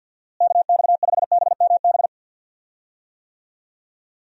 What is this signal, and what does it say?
Morse code, heard as a keyed tone, about 48 wpm, 700 Hz. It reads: KX5BMB
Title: USA Call Signs - 6 Characters - Repeated 2x - 50wpm